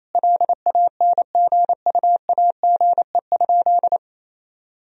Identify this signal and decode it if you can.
Morse code, heard as a keyed tone, about 28 wpm, 700 Hz. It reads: LANGUAGE?
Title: Sentences from Top 1000 Words - Review Part 3 - 28wpm